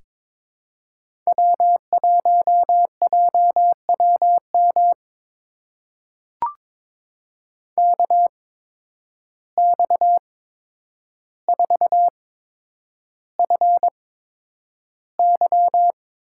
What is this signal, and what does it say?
Morse code, heard as a keyed tone, about 22 wpm, spaced as for 9 wpm, 700 Hz. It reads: W1JWM KX4FY
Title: USA Call Signs - 5 Characters - 8x Character Spacing - 22wpm